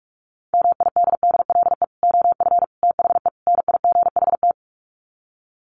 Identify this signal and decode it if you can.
Morse code, heard as a keyed tone, about 44 wpm, 700 Hz. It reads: MIDDLE OF THE NIGHT
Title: Sets of 4 Words 45wpm